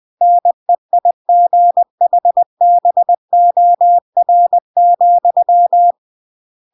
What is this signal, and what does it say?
Morse code, heard as a keyed tone, about 20 wpm, 700 Hz. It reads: NEIGHBOR,
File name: CW / jodorowsky / 82 commandments 20-20